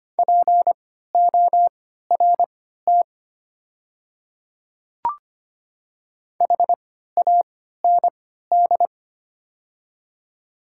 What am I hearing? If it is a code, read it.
Morse code, heard as a keyed tone, about 25 wpm, spaced as for 14 wpm, 700 Hz. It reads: PORT HAND
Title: 3x Character Spacing - Top 200 Words - 25wpm